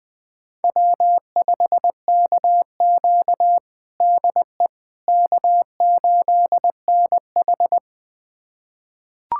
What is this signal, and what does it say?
Morse code, heard as a keyed tone, about 21 wpm, 700 Hz. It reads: W5KQ DE K8NH
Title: USA Callsign DE Callsign - 4 Characters - 20wpm